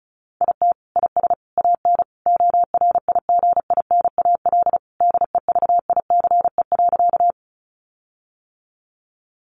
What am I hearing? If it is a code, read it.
Morse code, heard as a keyed tone, about 34 wpm, 700 Hz. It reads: IT IS AN ORIGINAL DEVICE.
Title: Sentences from Top 1500 Words 35wpm